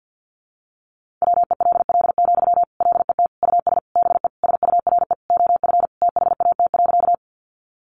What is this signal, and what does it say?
Morse code, heard as a keyed tone, about 50 wpm, 700 Hz. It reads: WELL, LET US BE SURE OF THAT.
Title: Sentences from Top 300 Words 50wpm